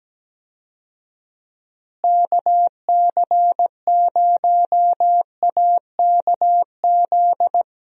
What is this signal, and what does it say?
Morse code, heard as a keyed tone, about 17 wpm, 700 Hz. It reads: KC0AKZ
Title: USA Call Signs - 6 Characters - Repeated 2x - 17wpm